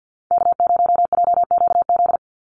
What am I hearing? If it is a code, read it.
Morse code, heard as a keyed tone, about 50 wpm, 700 Hz. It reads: K0JQZ
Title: NA SOTA Activator Callsigns 50wpm